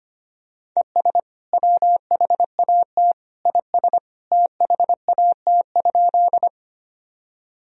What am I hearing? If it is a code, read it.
Morse code, heard as a keyed tone, about 25 wpm, 700 Hz. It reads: ES WHAT IS THAT?
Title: Sentences from Top 100 Words and Abbreviations 25wpm